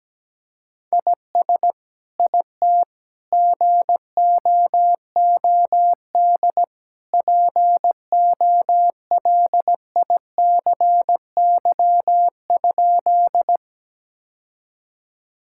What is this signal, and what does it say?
Morse code, heard as a keyed tone, about 17 wpm, 700 Hz. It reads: IS IT GOOD POLICY?
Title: Sentences from Top 1000 Words - Review Part 5 - 17wpm